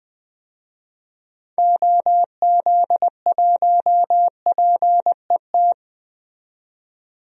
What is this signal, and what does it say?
Morse code, heard as a keyed tone, about 20 wpm, 700 Hz. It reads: OZ1PET
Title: DXCC-100 Call Signs - 6 Characters - 20wpm